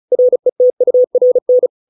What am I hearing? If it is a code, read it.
Morse code, heard as a keyed tone, about 34 wpm, 500 Hz. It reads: RETURN